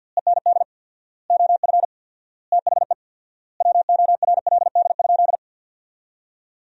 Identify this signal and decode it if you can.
Morse code, heard as a keyed tone, about 50 wpm, 700 Hz. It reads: END OF THE WORLD?
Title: 4x Word-Spacing - Sentences from Top 200 Words - 50wpm